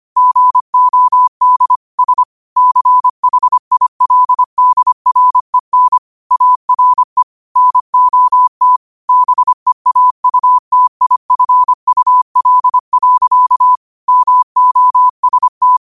Morse code, heard as a keyed tone, about 25 wpm, 1000 Hz. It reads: GOD S CHILDREN ARE NOT BEAUTIFUL. MOST
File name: qotdfiles / 25wpm